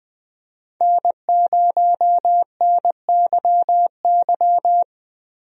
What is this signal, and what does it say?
Morse code, heard as a keyed tone, about 21 wpm, 700 Hz. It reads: N0NYY
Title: USA Call Signs - 5 Characters - 20wpm